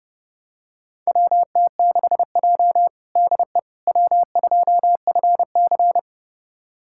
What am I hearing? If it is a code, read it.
Morse code, heard as a keyed tone, about 30 wpm, 700 Hz. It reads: WT6J DE W2FC
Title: USA Callsign DE Callsign - 4 Characters - 30wpm